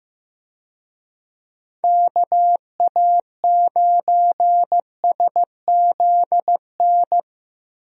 Morse code, heard as a keyed tone, about 15 wpm, 700 Hz. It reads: KA9SZN
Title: USA Call Signs - 6 Characters - Repeated 2x - 15wpm